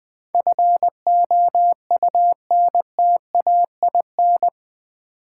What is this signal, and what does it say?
Morse code, heard as a keyed tone, about 20 wpm, 700 Hz. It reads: FOUNTAIN
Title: Easy Words with Hints - 20wpm